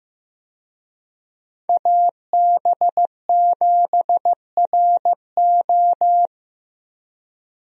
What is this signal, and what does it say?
Morse code, heard as a keyed tone, about 15 wpm, 700 Hz. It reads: AB7RO
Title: USA Call Signs - 5 Characters - Repeated 3x - 15wpm